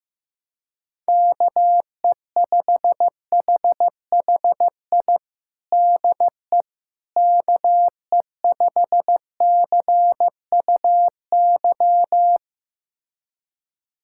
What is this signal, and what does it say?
Morse code, heard as a keyed tone, about 15 wpm, 700 Hz. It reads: KE5HHI DE KE5CUY
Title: USA Callsign DE Callsign - 6 Characters - 15wpm